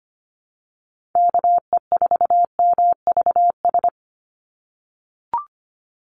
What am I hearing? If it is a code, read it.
Morse code, heard as a keyed tone, about 25 wpm, 690 Hz. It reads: KE4MVS
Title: USA Call Signs 25wpm